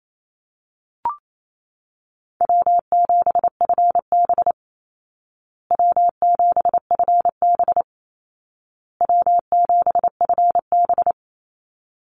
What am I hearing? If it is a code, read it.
Morse code, heard as a keyed tone, about 28 wpm, 700 Hz. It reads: W7FB W7FB W7FB
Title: Easy Call Signs Repeated 3x - 28wpm